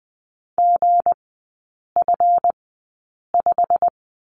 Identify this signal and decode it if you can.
Morse code, heard as a keyed tone, about 20 wpm, spaced as for 8 wpm, 690 Hz. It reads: GF5
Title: 159 Lesson 26e - Letter g - Three Characters 20wpm